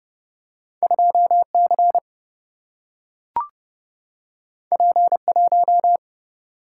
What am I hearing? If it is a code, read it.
Morse code, heard as a keyed tone, about 30 wpm, 700 Hz. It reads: 2C P1
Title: Single-Letter Single-Number 30wpm